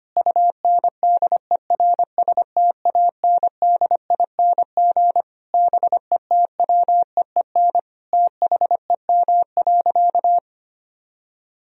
Morse code, heard as a keyed tone, about 25 wpm, 700 Hz. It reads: UNDERSTANDING BETWEEN THEM.
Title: Sentences from Top 1000 Words - Review Part 3 - 25wpm